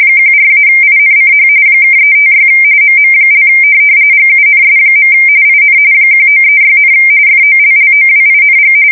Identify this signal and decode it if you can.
RTTY, about 45.45 baud. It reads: 14-INCH (356 MM) GUNS. IN 1936, FOLLOWING
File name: fsk170 / USS-Massachusetts